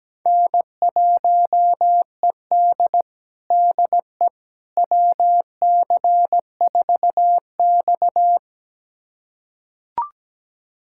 Morse code, heard as a keyed tone, about 17 wpm, 700 Hz. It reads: N1ED DE WC4X
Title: USA Callsign DE Callsign - 4 Characters - 17wpm